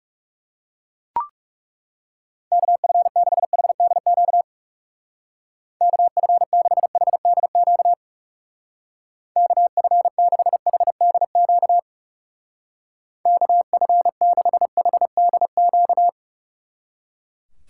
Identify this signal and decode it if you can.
Morse code, heard as a keyed tone, about 36 wpm, 700 Hz. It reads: KF6HDQ KF6HDQ KF6HDQ KF6HDQ
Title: Speed Racing - USA 2x3 Call Signs 30wpm